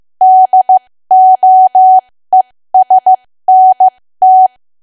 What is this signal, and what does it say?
Morse code, heard as a keyed tone, about 15 wpm, spaced as for 13 wpm, 750 Hz. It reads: DOESNT
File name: cwa / W1AW-Level2 / 13WPM